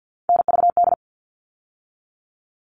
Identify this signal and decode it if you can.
Morse code, heard as a keyed tone, about 50 wpm, 700 Hz. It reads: NVD
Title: All Cut Numbers - 3-Digit 50wpm